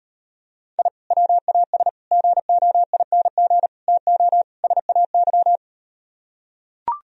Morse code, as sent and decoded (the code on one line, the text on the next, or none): .. / .-- .- ... / --. --- .. -. --. / - --- / ... .- -.--
I WAS GOING TO SAY